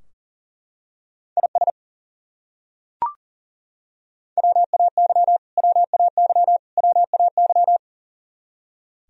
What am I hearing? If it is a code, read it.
Morse code, heard as a keyed tone, about 40 wpm, 700 Hz. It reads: IS WAY WAY WAY
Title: Top 100 Words Repeated 40wpm